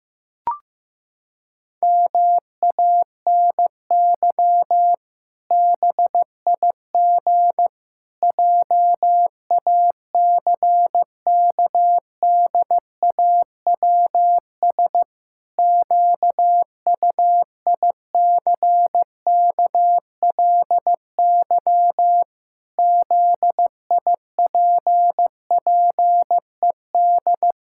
Morse code, heard as a keyed tone, about 15 wpm, 700 Hz. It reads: MANY BIG JACKDAWS QUICKLY ZIPPED
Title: Pangram Challenge v2 - 15wpm